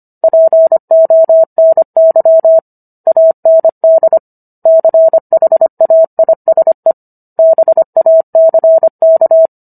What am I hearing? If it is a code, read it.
Morse code, heard as a keyed tone, about 25 wpm, 650 Hz. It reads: PONY AND CHAISE BACK